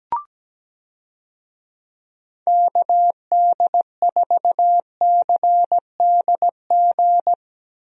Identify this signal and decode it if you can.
Morse code, heard as a keyed tone, about 17 wpm, 700 Hz. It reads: KD4CDG
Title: USA Call Signs - 6 Characters - Repeated 2x - 17wpm